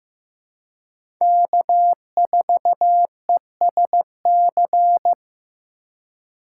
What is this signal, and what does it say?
Morse code, heard as a keyed tone, about 15 wpm, 700 Hz. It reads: K4ESC
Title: USA Call Signs - 5 Characters - 2x Character Spacing - 15wpm